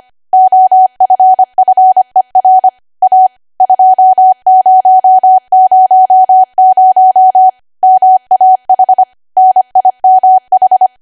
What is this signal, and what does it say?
Morse code, heard as a keyed tone, about 25 wpm, 750 Hz. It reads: OFFER A 2000 MAH NIMH
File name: Archive / 25 WPM / 25WPM